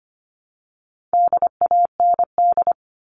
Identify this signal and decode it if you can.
Morse code, heard as a keyed tone, about 25 wpm, 700 Hz. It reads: DAND
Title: All Cut Numbers - 4-Digit 25wpm